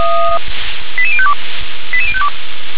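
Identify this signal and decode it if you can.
Five-tone selective call (ZVEI), heard as quick runs of five tones, tones of about 70 ms. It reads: 88052 88052
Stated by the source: ZVEI five-tone call received on SDR.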